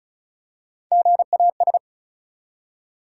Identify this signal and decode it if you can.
Morse code, heard as a keyed tone, about 34 wpm, 700 Hz. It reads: GAS
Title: Three Letter Words - 35wpm